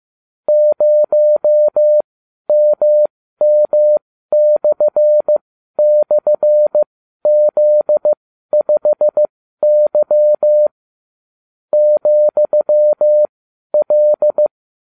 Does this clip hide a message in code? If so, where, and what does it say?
Yes: Morse code from 0.5 to 14.5 s, heard as a keyed tone, about 15 wpm, spaced as for 12 wpm, 600 Hz. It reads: 0MM//Z5Y ,L